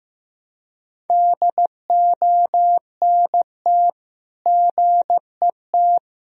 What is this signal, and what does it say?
Morse code, heard as a keyed tone, about 15 wpm, 700 Hz. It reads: DONT GET
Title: Sentences from Top 1000 Words - Review Part 4 - 15wpm